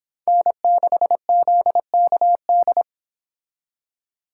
Morse code, heard as a keyed tone, about 25 wpm, 700 Hz. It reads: N6ZKD